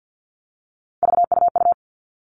Morse code, heard as a keyed tone, about 50 wpm, 700 Hz. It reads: VUU
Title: All Cut Numbers - 3-Digit 50wpm